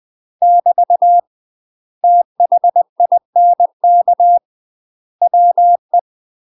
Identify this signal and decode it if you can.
Morse code, heard as a keyed tone, about 20 wpm, 700 Hz. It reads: <BT> THINK WE